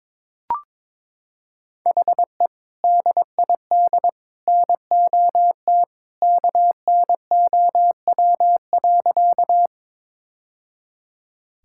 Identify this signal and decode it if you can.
Morse code, heard as a keyed tone, about 22 wpm, 700 Hz. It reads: HE DID NOT KNOW.
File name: practice / 22wpm / Sentences with 2-Word Phrases - 22wpm